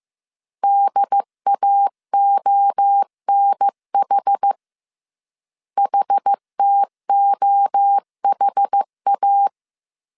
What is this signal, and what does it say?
Morse code, heard as a keyed tone, about 15 wpm, 800 Hz. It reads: DAONH HTOHA
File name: CW / DET / 1 3 1500C S A 14wpm 2447